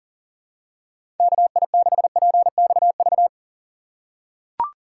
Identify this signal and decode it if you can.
Morse code, heard as a keyed tone, about 40 wpm, 700 Hz. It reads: KI6PXV
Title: USA Call Signs - 6 Characters - 4x Character Spacing - 40wpm